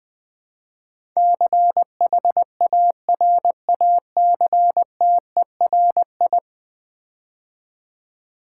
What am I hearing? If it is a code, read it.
Morse code, heard as a keyed tone, about 20 wpm, 700 Hz. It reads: CHARACTERI
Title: Build A Word - 5 Letters At A Time - 12-Letter Words - 20wpm